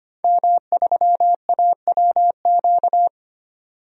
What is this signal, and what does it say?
Morse code, heard as a keyed tone, about 25 wpm, 700 Hz. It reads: M3AWQ
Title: DXCC-100 Call Signs - 5 Characters - 25wpm